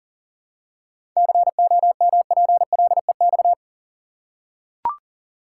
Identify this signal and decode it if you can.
Morse code, heard as a keyed tone, about 40 wpm, 700 Hz. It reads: COMPLEX
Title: Seven Letter Words - 40wpm